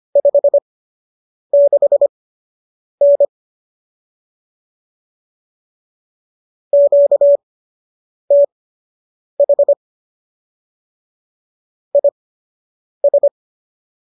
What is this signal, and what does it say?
Morse code, heard as a keyed tone, about 25 wpm, spaced as for 8 wpm, 570 Hz. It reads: 56N QTH IS
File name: qso006 08